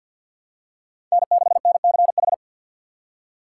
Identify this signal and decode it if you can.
Morse code, heard as a keyed tone, about 51 wpm, 700 Hz. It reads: N6NCH
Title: USA Call Signs - 5 Characters - 2x Character Spacing - 50wpm